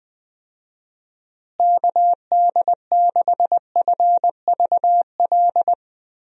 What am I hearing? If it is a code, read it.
Morse code, heard as a keyed tone, about 20 wpm, 700 Hz. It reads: KD6FVL
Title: USA Call Signs - 6 Characters - Repeated 2x - 20wpm